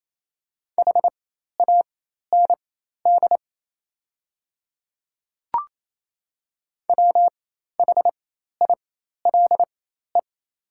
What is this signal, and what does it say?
Morse code, heard as a keyed tone, about 28 wpm, spaced as for 13 wpm, 700 Hz. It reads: HAND WHILE
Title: 4x Character Spacing - Top 300 Words - 28wpm